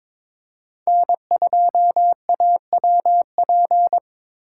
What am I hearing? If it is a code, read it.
Morse code, heard as a keyed tone, about 22 wpm, 700 Hz. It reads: N2AWP
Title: USA Call Signs - 5 Characters - 3x Character Spacing - 22wpm